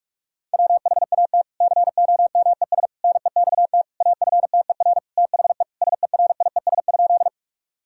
Morse code, heard as a keyed tone, about 46 wpm, 700 Hz. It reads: WHAT COMES NEXT AFTER THE SERIES?
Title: Sentences from Top 1000 Words - Review Part 5 - 45wpm